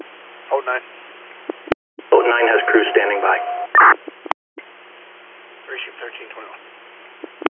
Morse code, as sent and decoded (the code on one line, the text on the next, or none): none